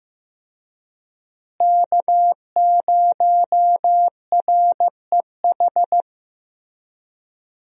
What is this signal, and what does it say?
Morse code, heard as a keyed tone, about 15 wpm, 700 Hz. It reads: K0REH
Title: USA Call Signs - 5 Characters - Repeated 2x - 15wpm